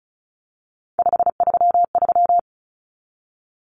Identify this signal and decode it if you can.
Morse code, heard as a keyed tone, about 34 wpm, 700 Hz. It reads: 533
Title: Numbers - 3-Digit 35wpm